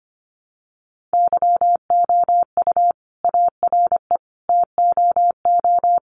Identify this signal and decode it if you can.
Morse code, heard as a keyed tone, about 25 wpm, 700 Hz. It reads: YOU ARE TOO
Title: Sentences from Top 1400 Words 25wpm